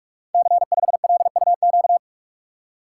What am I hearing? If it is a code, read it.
Morse code, heard as a keyed tone, about 42 wpm, 700 Hz. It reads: C5LUQ
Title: Code Groups - 5 Letter:Numbers - 45wpm